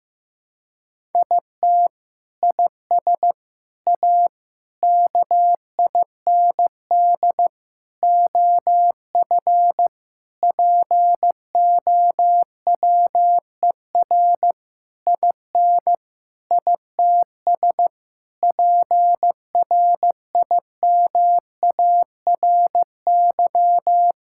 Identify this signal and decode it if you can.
Morse code, heard as a keyed tone, about 15 wpm, 700 Hz. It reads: IT IS A KIND OF POWER IN ITS PRIMARY
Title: Sentences from Top 1000 Words - Review Part 5 - 15wpm